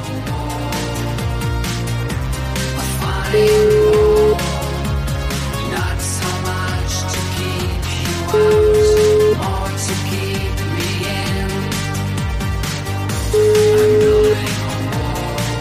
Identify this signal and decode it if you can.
Ringback tone (Europe, 425 Hz), with noise.